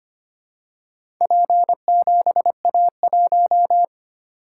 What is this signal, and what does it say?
Morse code, heard as a keyed tone, about 25 wpm, 700 Hz. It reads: P7A1
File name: practice / 25wpm / Four Letter-Number Combinations - 25wpm